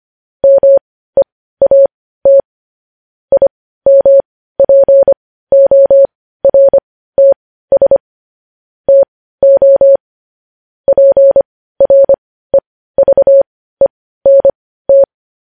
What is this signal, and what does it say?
Morse code, heard as a keyed tone, about 25 wpm, spaced as for 15 wpm, 550 Hz. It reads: MEAT IMPORTS TO PREVENT
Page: news-headlines-in-morse-code-at-15-wpm